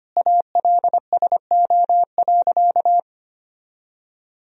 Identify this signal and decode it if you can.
Morse code, heard as a keyed tone, about 25 wpm, 700 Hz. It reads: ALSO.
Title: Sentences from Top 1000 Words - Review Part 3 - 25wpm